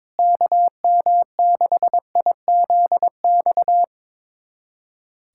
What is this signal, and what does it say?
Morse code, heard as a keyed tone, about 21 wpm, 700 Hz. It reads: KM6IZX
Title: USA Callsign DE Callsign - 6 Characters - 22wpm